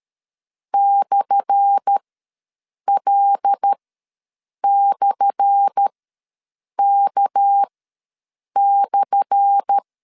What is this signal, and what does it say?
Morse code, heard as a keyed tone, about 13 wpm, spaced as for 7 wpm, 800 Hz. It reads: /L/K/
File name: LEZIONI / 10 SIGNS / 10A SLASH ws 12wpm 3443